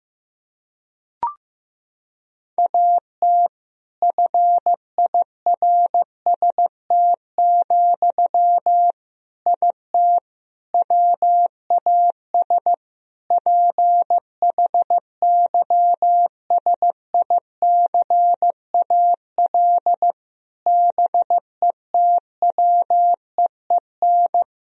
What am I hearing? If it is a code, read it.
Morse code, heard as a keyed tone, about 15 wpm, 700 Hz. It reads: AT FIRST, IT WAS PHYSICAL BETWEEN
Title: Sentences from Top 1000 Words - Review Part 4 - 15wpm